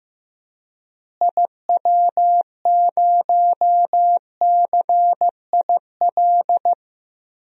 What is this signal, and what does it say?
Morse code, heard as a keyed tone, about 15 wpm, 700 Hz. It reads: IW0CIL
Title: DXCC-100 Call Signs - 6 Characters - Repeated 2x - 15wpm